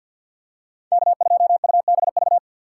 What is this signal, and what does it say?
Morse code, heard as a keyed tone, about 53 wpm, 700 Hz. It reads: K2UBV
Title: USA Call Signs - 5 Characters - 6x Character Spacing - 50wpm